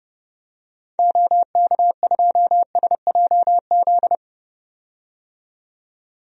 Morse code, heard as a keyed tone, about 30 wpm, 700 Hz. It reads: OK2SJZ
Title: DXCC-100 Call Signs - 6 Characters - 30wpm